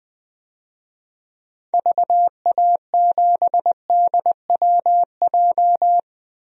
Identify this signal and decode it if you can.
Morse code, heard as a keyed tone, about 20 wpm, 700 Hz. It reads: VA7DWJ
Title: DXCC-100 Call Signs - 6 Characters - 20wpm